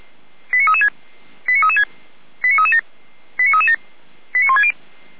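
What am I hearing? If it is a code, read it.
Five-tone selective call (ZVEI), heard as quick runs of five tones, tones of about 70 ms. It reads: 89336 89336 89337 89337 89160
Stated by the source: ZVEI five-tone selective call 89165 received via SDR.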